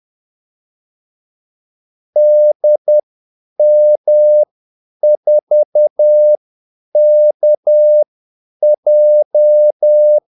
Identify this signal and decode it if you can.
Morse code, heard as a keyed tone, about 10 wpm, spaced as for 8 wpm, 600 Hz. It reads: DM4KJ
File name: Callsigns-10-8wpm